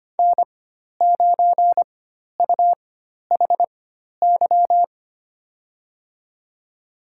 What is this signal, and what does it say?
Morse code, heard as a keyed tone, about 25 wpm, spaced as for 12 wpm, 700 Hz. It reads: N9UHY
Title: USA Call Signs - 5 Characters - 4x Character Spacing - 25wpm